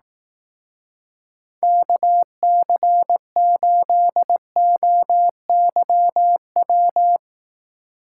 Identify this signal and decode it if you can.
Morse code, heard as a keyed tone, about 18 wpm, 700 Hz. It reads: KC8OYW
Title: Speed Racing - USA 2x3 Call Signs 15wpm